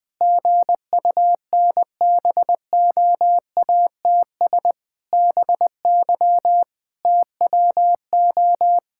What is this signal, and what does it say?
Morse code, heard as a keyed tone, about 20 wpm, 700 Hz. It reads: GUNBOATS BY TWO
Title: Pangram Challenge v2 - 20wpm